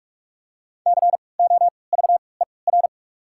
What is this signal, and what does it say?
Morse code, heard as a keyed tone, about 43 wpm, spaced as for 25 wpm, 700 Hz. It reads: COVER
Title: 3x Character Spacing - Top 300 Words - 45wpm